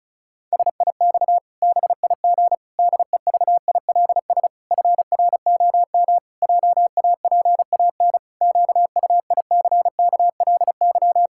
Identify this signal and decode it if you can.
Morse code, heard as a keyed tone, about 36 wpm, 700 Hz. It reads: SIX BIG DEVILS FROM JAPAN QUICKLY
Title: Pangram Challenge v2 - 35wpm